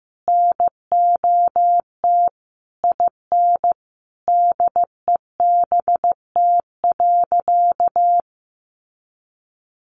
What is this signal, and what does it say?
Morse code, heard as a keyed tone, about 15 wpm, 700 Hz. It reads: NOT IN DEBT.
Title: Sentences from Top 1300 Words 15wpm